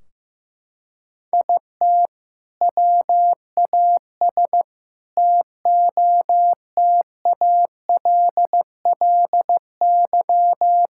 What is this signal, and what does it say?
Morse code, heard as a keyed tone, about 16 wpm, 700 Hz. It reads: IT WAS TOTALLY
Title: Sentences from Top 1600 Words 15wpm